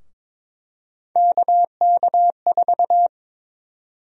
Morse code, heard as a keyed tone, about 22 wpm, 700 Hz. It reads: KK4
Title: USA Call Sign Prefixes - Repeated 2x - 22wpm